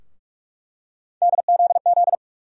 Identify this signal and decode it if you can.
Morse code, heard as a keyed tone, about 46 wpm, 700 Hz. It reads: DZZ
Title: USA Call Sign Postfixes - 3 Letters 45wpm